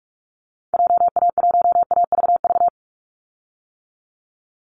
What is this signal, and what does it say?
Morse code, heard as a keyed tone, about 44 wpm, 700 Hz. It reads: JA1AVV
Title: USA Call Signs 45wpm